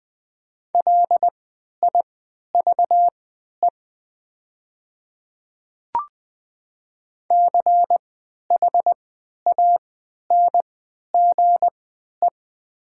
Morse code, heard as a keyed tone, about 20 wpm, spaced as for 11 wpm, 700 Hz. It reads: LIVE CHANGE
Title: 3x Character Spacing - Top 200 Words - 20wpm